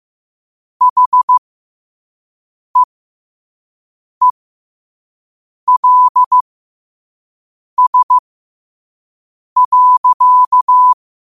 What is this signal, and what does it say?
Morse code, heard as a keyed tone, about 15 wpm, spaced as for 5 wpm, 1000 Hz. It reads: HEELS.
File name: qotdfiles / 5wpm